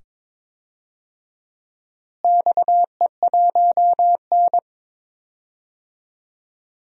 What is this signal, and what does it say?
Morse code, heard as a keyed tone, about 22 wpm, 700 Hz. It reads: XE1N
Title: DXCC-100 Call Signs - 4 Characters - Repeated 2x - 22wpm